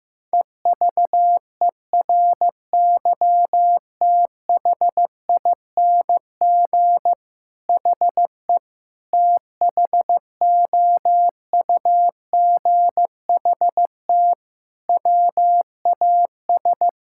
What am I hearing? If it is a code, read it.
Morse code, heard as a keyed tone, about 15 wpm, 700 Hz. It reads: EVERYTHING HE THOUGHT WAS
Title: Sentences from Top 1000 Words - Review Part 6 - 15wpm